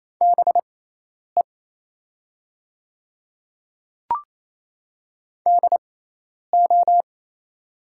Morse code, heard as a keyed tone, about 28 wpm, spaced as for 10 wpm, 700 Hz. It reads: BE DO
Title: 6x Character Spacing - Top 100 Words - 28wpm